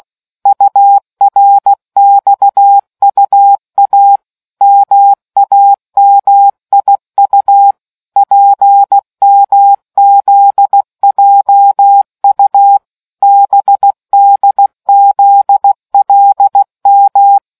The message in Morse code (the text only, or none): URXUA MAMIU PMZJU BDZLM